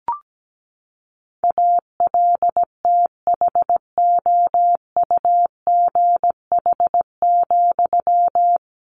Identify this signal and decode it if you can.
Morse code, heard as a keyed tone, about 17 wpm, 700 Hz. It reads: ALTHOUGH,
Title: Sentences from Top 600 Words 17wpm